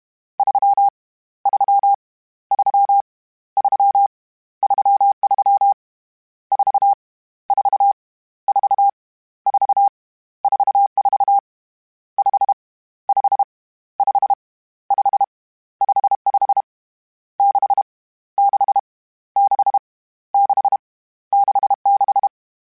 Morse code, heard as a keyed tone, about 31 wpm, 800 Hz. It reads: 3 3 3 3 33 4 4 4 4 44 5 5 5 5 55 6 6 6 6 66
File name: Misc / Misc-mp3s / Numb @30